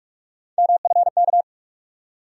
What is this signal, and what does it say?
Morse code, heard as a keyed tone, about 41 wpm, 700 Hz. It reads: MFK
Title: USA Call Sign Postfixes - 3 Letters 45wpm